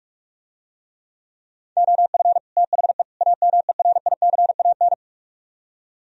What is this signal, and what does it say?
Morse code, heard as a keyed tone, about 46 wpm, 700 Hz. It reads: OF THE AMERICAN